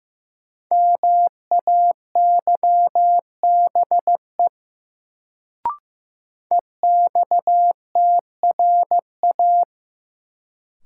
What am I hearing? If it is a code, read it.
Morse code, heard as a keyed tone, about 15 wpm, 700 Hz. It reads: MAYBE EXTRA
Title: Sentences from Top 1000 Words - New Word Drill - 15wpm